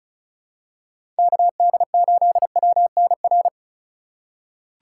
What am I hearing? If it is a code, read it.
Morse code, heard as a keyed tone, about 36 wpm, 700 Hz. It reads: KD8WNR
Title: USA Call Signs - 6 Characters - 4x Character Spacing - 35wpm